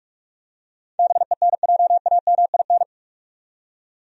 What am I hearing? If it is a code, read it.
Morse code, heard as a keyed tone, about 46 wpm, 700 Hz. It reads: BENJAMIN
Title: USA Common Names - 45wpm